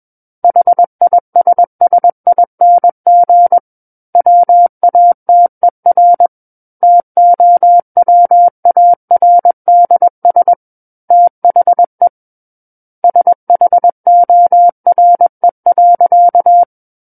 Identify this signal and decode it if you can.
Morse code, heard as a keyed tone, about 21 wpm, 700 Hz. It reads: HISSING WATER TOWARDS THE SHORE.
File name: Book One / 12-Chapter 12 - 21 WPM